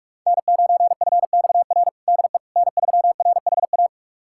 Morse code, heard as a keyed tone, about 45 wpm, 700 Hz. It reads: N9FXR DE N3RHA
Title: USA Callsign DE Callsign - 5 Characters - 45wpm